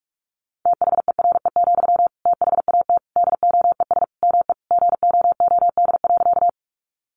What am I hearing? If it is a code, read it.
Morse code, heard as a keyed tone, about 44 wpm, 700 Hz. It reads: THERE, THAT DOES ME GOOD.
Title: Sentences from Top 200 Words 45wpm